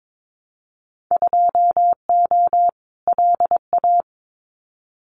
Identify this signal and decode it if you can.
Morse code, heard as a keyed tone, about 22 wpm, 700 Hz. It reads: 2O LA
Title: Winter Field Day Exchanges 22wpm